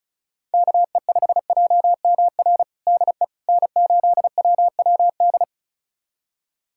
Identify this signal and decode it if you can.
Morse code, heard as a keyed tone, about 34 wpm, 700 Hz. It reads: KE5JMR DE N8WWD
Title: USA Callsign DE Callsign 35wpm